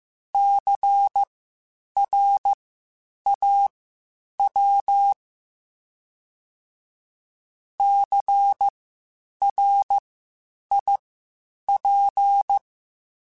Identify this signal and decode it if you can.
Morse code, heard as a keyed tone, about 15 wpm, spaced as for 8 wpm, 800 Hz. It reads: CRAW CRIP
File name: Spaced Alphas / Spaced Alphas mp3s / Cs sp @13